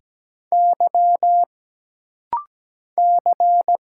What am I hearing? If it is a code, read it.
Morse code, heard as a keyed tone, about 17 wpm, 700 Hz. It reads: Y C
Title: Single Letters - Rapid-Fire 17wpm